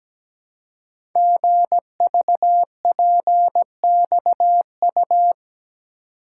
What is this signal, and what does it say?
Morse code, heard as a keyed tone, about 17 wpm, 700 Hz. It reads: GVPXU
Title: Code Groups - 5 Letters - 17wpm